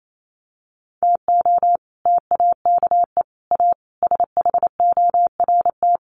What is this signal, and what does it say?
Morse code, heard as a keyed tone, about 28 wpm, 700 Hz. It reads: TO TAKE A SHORT
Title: Sets of 5 Words Encore2 28wpm